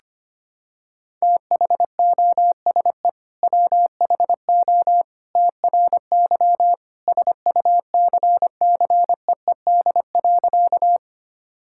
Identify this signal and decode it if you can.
Morse code, heard as a keyed tone, about 25 wpm, 700 Hz. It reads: THOSE WHO TRY SUCCEED.
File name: practice / 25wpm / Sentences with 2-Word Phrases - 25wpm